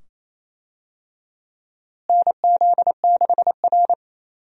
Morse code, heard as a keyed tone, about 28 wpm, 700 Hz. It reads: NZ6R
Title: USA Call Signs - 4 Characters - Repeated 2x - 28wpm